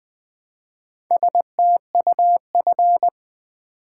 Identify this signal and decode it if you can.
Morse code, heard as a keyed tone, about 20 wpm, 700 Hz. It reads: STUF
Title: Build A Word - 1 Letter At A Time - 5-Letter Words - 20wpm